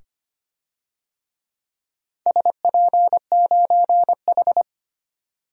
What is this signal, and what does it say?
Morse code, heard as a keyed tone, about 25 wpm, 700 Hz. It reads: SP9H
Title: DXCC-100 Call Signs - 4 Characters - 25wpm